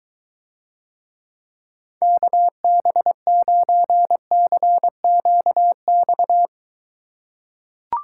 Morse code, heard as a keyed tone, about 23 wpm, 700 Hz. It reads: KB9CQX